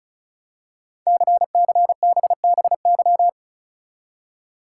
Morse code, heard as a keyed tone, about 36 wpm, 700 Hz. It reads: CCBBY
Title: Code Groups - 5 Letters - 35wpm